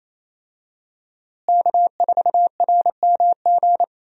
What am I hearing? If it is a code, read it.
Morse code, heard as a keyed tone, about 28 wpm, 700 Hz. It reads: K4RMG
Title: USA Call Signs - 5 Characters - Repeated 3x - 28wpm